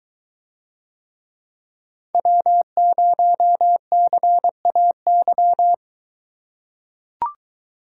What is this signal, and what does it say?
Morse code, heard as a keyed tone, about 22 wpm, 700 Hz. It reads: W0CAY